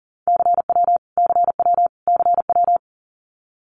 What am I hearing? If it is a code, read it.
Morse code, heard as a keyed tone, about 40 wpm, 700 Hz. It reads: CW CW CW
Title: Abbreviations Repeated 3x - 40wpm